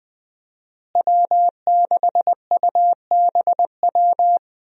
Morse code, heard as a keyed tone, about 20 wpm, 700 Hz. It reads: W6UBW
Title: USA Call Signs - 5 Characters - Repeated 3x - 20wpm